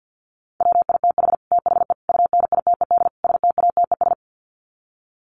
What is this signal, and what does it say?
Morse code, heard as a keyed tone, about 48 wpm, 700 Hz. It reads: WITH THE UNITED STATES
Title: Sets of 4 Words 50wpm